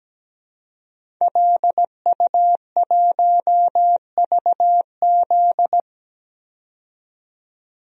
Morse code, heard as a keyed tone, about 17 wpm, 700 Hz. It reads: LU1VZ
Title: DXCC-100 Call Signs - 5 Characters - 17wpm